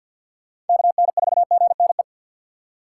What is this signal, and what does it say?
Morse code, heard as a keyed tone, about 46 wpm, 700 Hz. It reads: KN4GNE
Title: USA Call Signs - 6x Character Spacing - 50wpm